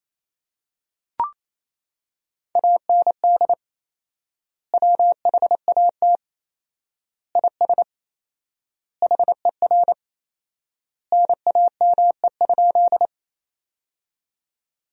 Morse code, heard as a keyed tone, about 28 wpm, 700 Hz. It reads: AND WHAT IS HER NAME?
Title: 4x Word-Spacing - Sentences from Top 200 Words - 28wpm